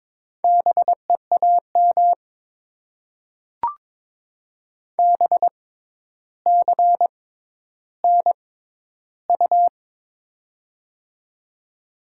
Morse code, heard as a keyed tone, about 22 wpm, spaced as for 11 wpm, 700 Hz. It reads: BEAM BCNU
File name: Top 90 QSO Elements - 6x Character Spacing - 22wpm